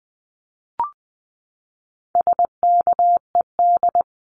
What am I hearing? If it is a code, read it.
Morse code, heard as a keyed tone, about 20 wpm, 700 Hz. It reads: SKED
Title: Abbreviations 20wpm